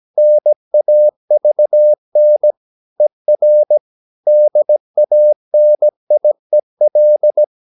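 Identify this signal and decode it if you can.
Morse code, heard as a keyed tone, about 17 wpm, 600 Hz. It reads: NAVN ER DANIEL
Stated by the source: Slow (QRS) Morse practice at 17 wpm, from CW med Gnister.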